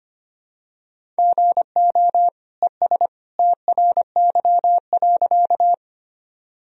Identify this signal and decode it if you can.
Morse code, heard as a keyed tone, about 25 wpm, 700 Hz. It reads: GO ES TRY.
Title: Sentences from Top 200 Words and Abbreviations 25wpm